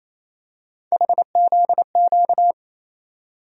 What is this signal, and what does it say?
Morse code, heard as a keyed tone, about 28 wpm, 700 Hz. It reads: HZQ
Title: USA Call Sign Postfixes - 3 Letters 28wpm